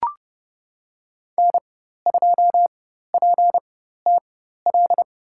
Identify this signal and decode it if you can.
Morse code, heard as a keyed tone, about 30 wpm, spaced as for 14 wpm, 700 Hz. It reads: N2PTL
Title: USA Call Signs - 4x Character Spacing - 30wpm